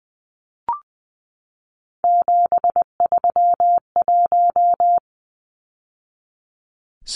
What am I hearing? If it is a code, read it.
Morse code, heard as a keyed tone, about 20 wpm, 700 Hz. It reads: 731
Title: Numbers - 3-Digit 20wpm